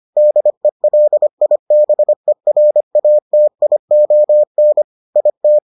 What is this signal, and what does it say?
Morse code, heard as a keyed tone, about 25 wpm, 600 Hz. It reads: DELIBERATION IT